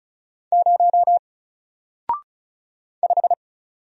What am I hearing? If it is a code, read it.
Morse code, heard as a keyed tone, about 36 wpm, 700 Hz. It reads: 0 5
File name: Single-Letter-Number-35wpm